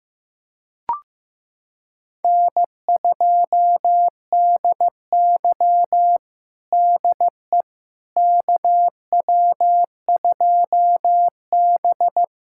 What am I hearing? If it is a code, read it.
Morse code, heard as a keyed tone, about 15 wpm, 700 Hz. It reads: N2DY DE KW2B
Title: USA Callsign DE Callsign - 4 Characters - 15wpm